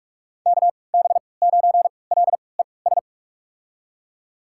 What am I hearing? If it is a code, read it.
Morse code, heard as a keyed tone, about 45 wpm, spaced as for 25 wpm, 700 Hz. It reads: KB9LES
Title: USA Call Signs - 6 Characters - 3x Character Spacing - 45wpm